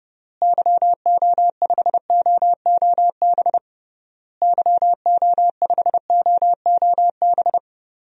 Morse code, heard as a keyed tone, about 32 wpm, 700 Hz. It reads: YO5OOB YO5OOB
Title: DXCC-100 Call Signs - 6 Characters - Repeated 3x - 30wpm